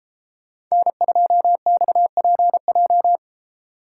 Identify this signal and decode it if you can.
Morse code, heard as a keyed tone, about 33 wpm, 700 Hz. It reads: N2XPJ
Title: Speed Racing - USA 1x3 Call Signs 28wpm